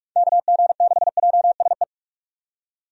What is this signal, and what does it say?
Morse code, heard as a keyed tone, about 46 wpm, 700 Hz. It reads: KG6JSE
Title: USA Call Signs - 6 Characters - 45wpm